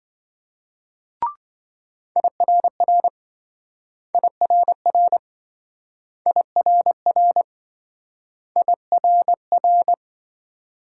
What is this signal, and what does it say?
Morse code, heard as a keyed tone, about 27 wpm, 700 Hz. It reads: IRR IRR IRR IRR